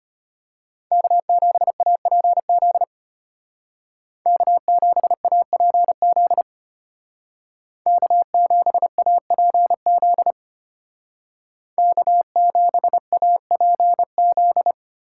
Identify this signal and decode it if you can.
Morse code, heard as a keyed tone, about 33 wpm, 700 Hz. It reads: K7APZ K7APZ K7APZ K7APZ